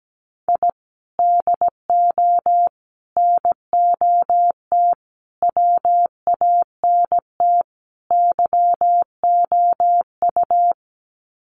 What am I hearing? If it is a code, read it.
Morse code, heard as a keyed tone, about 17 wpm, 700 Hz. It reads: I DO NOT WANT YOU
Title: Sets of 5 Words 17wpm